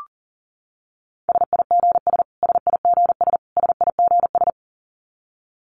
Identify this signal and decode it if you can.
Morse code, heard as a keyed tone, about 40 wpm, 700 Hz. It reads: SIGS SIGS SIGS
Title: Abbreviations Repeated 3x - 40wpm